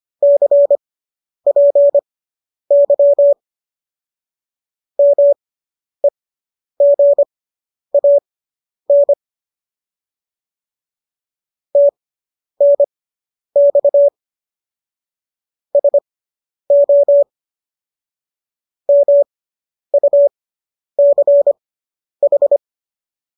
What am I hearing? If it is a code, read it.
Morse code, heard as a keyed tone, about 25 wpm, spaced as for 10 wpm, 570 Hz. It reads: CPY MEGAN TNX SO MUCH